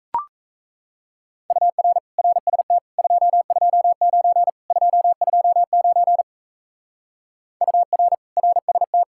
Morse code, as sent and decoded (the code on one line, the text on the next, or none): ..- .-. / .-. ... - / ..--- ..--- ----. / ..--- ..--- ----. / ..- .-. / .-. ... -
UR RST 229 229 UR RST